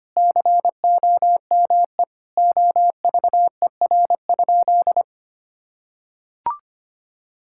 Morse code, heard as a keyed tone, about 25 wpm, 700 Hz. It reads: COME OVER?
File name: practice / 25wpm / Sentences with 4-Word Phrases - 25wpm